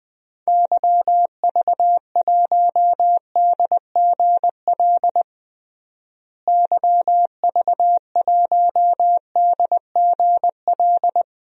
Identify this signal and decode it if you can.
Morse code, heard as a keyed tone, about 20 wpm, 700 Hz. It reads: YV1DGL YV1DGL
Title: DXCC-100 Call Signs - 6 Characters - Repeated 3x - 20wpm